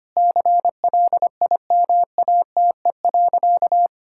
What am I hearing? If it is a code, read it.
Morse code, heard as a keyed tone, about 25 wpm, 700 Hz. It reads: CLIMATE.
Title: Sentences from Top 1600 Words 25wpm